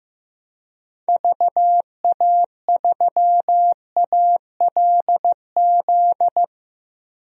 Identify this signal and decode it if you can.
Morse code, heard as a keyed tone, about 15 wpm, 700 Hz. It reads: VA3ALZ
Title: DXCC-100 Call Signs - 6 Characters - 15wpm